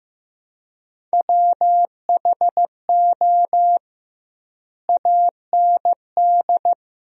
Morse code, heard as a keyed tone, about 15 wpm, 700 Hz. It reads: WHO AND
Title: 2x Word-Spacing - Sentences from Top 100 Words - 15wpm